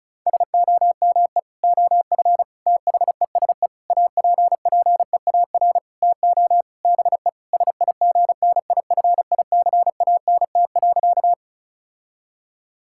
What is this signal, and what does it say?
Morse code, heard as a keyed tone, about 36 wpm, 700 Hz. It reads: SOME OF THESE APPEAR TO BE SIGNIFICANT.
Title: Sentences from Top 1000 Words - Review Part 6 - 35wpm